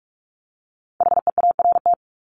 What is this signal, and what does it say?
Morse code, heard as a keyed tone, about 44 wpm, 700 Hz. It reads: HEART
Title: Top 1001-1250 Words 45wpm